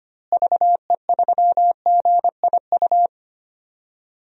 Morse code, heard as a keyed tone, about 25 wpm, 700 Hz. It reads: VE3GIU
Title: DXCC-100 Call Signs - 6 Characters - Repeated 3x - 25wpm